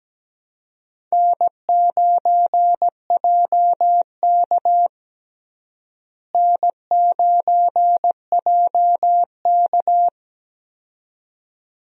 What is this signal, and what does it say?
Morse code, heard as a keyed tone, about 17 wpm, 700 Hz. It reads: N9JK N9JK
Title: USA Call Signs - 4 Characters - Repeated 2x - 17wpm